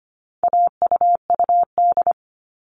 Morse code, heard as a keyed tone, about 25 wpm, 700 Hz. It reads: AUUD
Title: All Cut Numbers - 4-Digit 25wpm